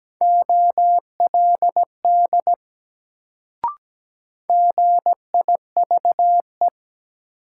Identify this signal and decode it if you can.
Morse code, heard as a keyed tone, about 17 wpm, 700 Hz. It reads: OLD GIVE
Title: Sets of 2 Words - New Word Drill - 17wpm